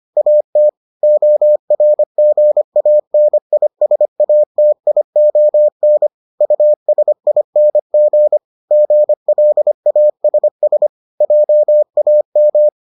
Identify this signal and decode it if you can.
Morse code, heard as a keyed tone, about 25 wpm, 600 Hz. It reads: AT ORGANISATION USING GLASS JAM